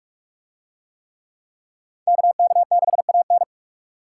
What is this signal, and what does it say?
Morse code, heard as a keyed tone, about 44 wpm, 700 Hz. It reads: KK6AN